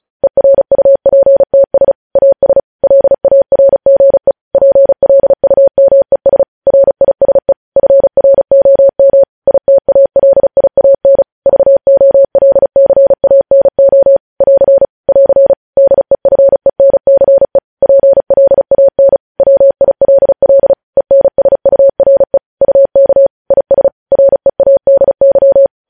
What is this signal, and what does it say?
Morse code, heard as a keyed tone, about 36 wpm, 550 Hz. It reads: ERUPTS AS LARGE PLUMES RISE FROM ITALIAN VOLCANO <AR> <AR> DEFENCE PLAN WILL ENSURE UK IS READY
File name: podcast / News headlines-1x-35x35